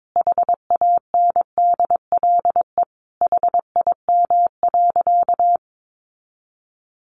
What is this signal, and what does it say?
Morse code, heard as a keyed tone, about 22 wpm, 700 Hz. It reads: HANDLE HIM.
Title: Sentences from Top 1100 Words 22wpm